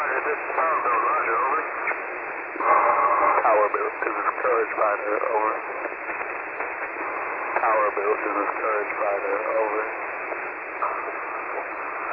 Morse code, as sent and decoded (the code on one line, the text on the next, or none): none